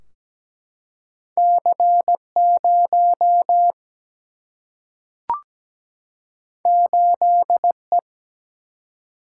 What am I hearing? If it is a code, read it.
Morse code, heard as a keyed tone, about 17 wpm, 700 Hz. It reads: C 0 8 E
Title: Single-Letter Single-Number 17wpm